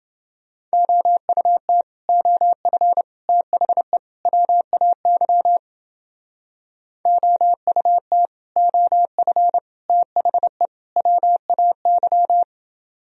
Morse code, heard as a keyed tone, about 30 wpm, 700 Hz. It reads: OUT OF THE WAY OUT OF THE WAY